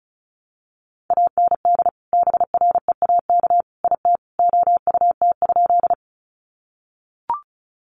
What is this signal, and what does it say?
Morse code, heard as a keyed tone, about 34 wpm, 700 Hz. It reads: AND BREAK IT OUT?
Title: Sentences from Top 900 Words 35wpm